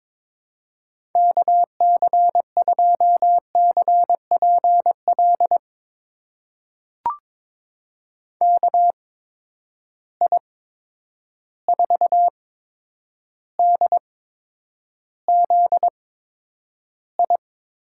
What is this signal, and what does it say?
Morse code, heard as a keyed tone, about 22 wpm, spaced as for 9 wpm, 700 Hz. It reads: KC2CPL KI4DZI
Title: USA Call Signs - 8x Character Spacing - 22wpm